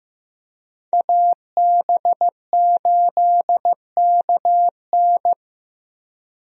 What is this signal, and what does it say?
Morse code, heard as a keyed tone, about 15 wpm, 700 Hz. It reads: AB8KN
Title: USA Call Signs - 5 Characters - Repeated 2x - 15wpm